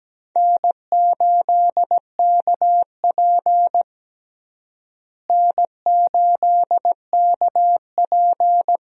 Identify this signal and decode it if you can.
Morse code, heard as a keyed tone, about 17 wpm, 700 Hz. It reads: N8KP N8KP
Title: USA Call Signs - 4 Characters - Repeated 2x - 17wpm